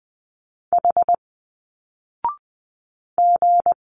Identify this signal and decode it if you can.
Morse code, heard as a keyed tone, about 20 wpm, 700 Hz. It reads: H G
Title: Single Letters 20wpm